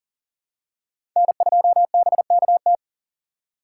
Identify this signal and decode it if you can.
Morse code, heard as a keyed tone, about 40 wpm, 700 Hz. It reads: N2BKT
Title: USA Call Signs - 6x Character Spacing - 40wpm